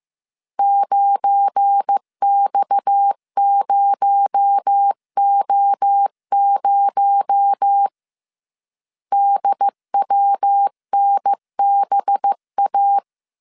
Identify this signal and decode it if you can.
Morse code, heard as a keyed tone, about 15 wpm, 800 Hz. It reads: 9X0O0 DWNBA
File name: CW / DET / 1 9 1500C S A 14wpm 3222